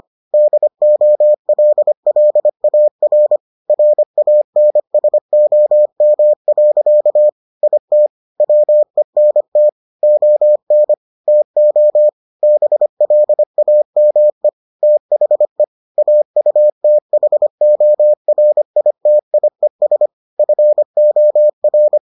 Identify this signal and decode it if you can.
Morse code, heard as a keyed tone, about 25 wpm, 600 Hz. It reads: DOLLAR RANSOM. IT WENT ON TO BLAME THE AUTHORITIES FOR